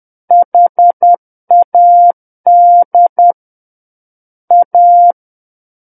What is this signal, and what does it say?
Morse code, heard as a keyed tone, about 10 wpm, 700 Hz. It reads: HAD A
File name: Book One / 01-Chapter 1 - 10 WPM